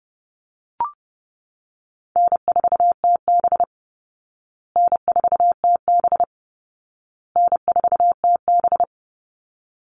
Morse code, heard as a keyed tone, about 30 wpm, 700 Hz. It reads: N4TB N4TB N4TB
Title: Easy Call Signs Repeated 3x - 30wpm